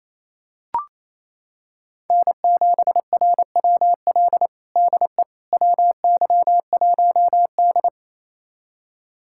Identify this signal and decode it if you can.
Morse code, heard as a keyed tone, about 28 wpm, 700 Hz. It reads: N7RWL DE WY1D
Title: USA Callsign DE Callsign 28wpm